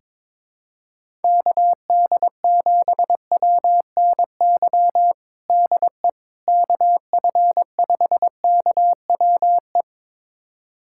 Morse code, heard as a keyed tone, about 22 wpm, 700 Hz. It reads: KD7WNY DE KF5KWE
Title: USA Callsign DE Callsign - 6 Characters - 22wpm